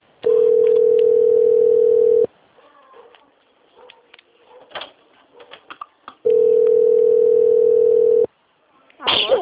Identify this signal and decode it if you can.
Ringback tone (North America), 440 + 480 Hz, with speech.
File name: rg-1600-unknown- - - .35701